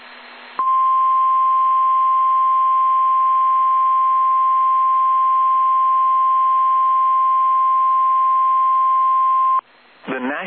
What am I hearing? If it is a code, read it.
Emergency Alert System (SAME) tones (the 1050 Hz attention tone). It is heard from 0.5 to 9.6 s.